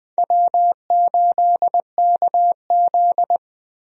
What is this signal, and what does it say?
Morse code, heard as a keyed tone, about 20 wpm, 700 Hz. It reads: W8KZ
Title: USA Call Signs - 4 Characters - Repeated 2x - 20wpm